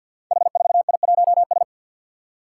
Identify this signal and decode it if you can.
Morse code, heard as a keyed tone, about 47 wpm, 700 Hz. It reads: H4I1S
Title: Five Challenging Letter-Number Combinations 50wpm